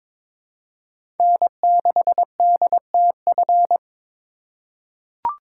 Morse code, heard as a keyed tone, about 22 wpm, 700 Hz. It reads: N6DTF
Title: USA Call Signs - 4x Character Spacing - 22wpm